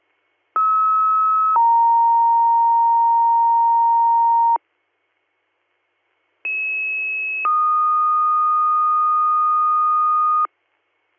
Two-tone page, heard as a one-second tone, then a three-second tone, about 1290 then 930 Hz, about 2600 then 1230 Hz. It is heard from 0.6 to 10.5 s.